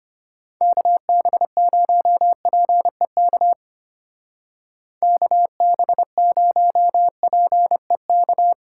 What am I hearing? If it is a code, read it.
Morse code, heard as a keyed tone, about 31 wpm, 700 Hz. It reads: KB0PEK KB0PEK
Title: Speed Racing - USA Call Signs 25wpm